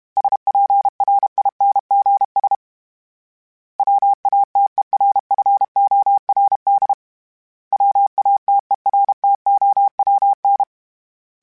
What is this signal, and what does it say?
Morse code, heard as a keyed tone, about 32 wpm, 800 Hz. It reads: SPRINGS WATERFORD WATERTOWN
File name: fromSMs / FromSMs mp3s / Cities@30